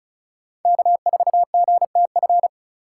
Morse code, heard as a keyed tone, about 34 wpm, 700 Hz. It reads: K4GTF
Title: USA Call Signs - 5 Characters - 2x Character Spacing - 35wpm